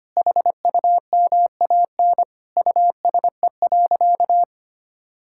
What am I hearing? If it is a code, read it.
Morse code, heard as a keyed tone, about 25 wpm, 700 Hz. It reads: HUMAN USE.
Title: Sentences from Top 1000 Words - Review Part 6 - 25wpm